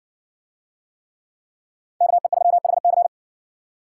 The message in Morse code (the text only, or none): KE3HC